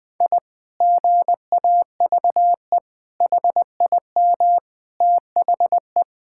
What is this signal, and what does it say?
Morse code, heard as a keyed tone, about 20 wpm, 700 Hz. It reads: I GAVE HIM THE
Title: Sentences from Top 1000 Words - Review Part 6 - 20wpm